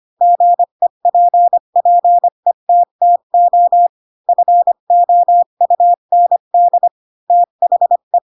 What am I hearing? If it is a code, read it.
Morse code, heard as a keyed tone, about 25 wpm, 700 Hz. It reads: GEPPETTO FOUND THE